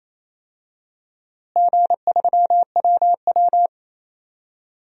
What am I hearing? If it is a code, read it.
Morse code, heard as a keyed tone, about 28 wpm, 700 Hz. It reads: G3WW
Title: DXCC-100 Call Signs - 4 Characters - Repeated 3x - 28wpm